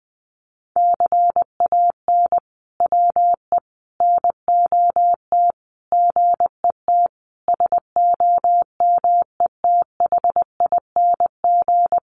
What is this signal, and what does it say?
Morse code, heard as a keyed tone, about 20 wpm, 700 Hz. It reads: CAN WE NOT GET SOMETHING
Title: Sentences from Top 1500 Words 20wpm